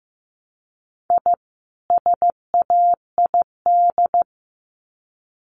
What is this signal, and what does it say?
Morse code, heard as a keyed tone, about 15 wpm, 700 Hz. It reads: I SAID
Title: Sets of 2 Words 15wpm